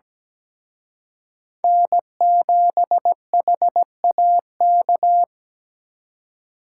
Morse code, heard as a keyed tone, about 20 wpm, 700 Hz. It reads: N7HAK
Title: Speed Racing - USA 1x3 Call Signs 17wpm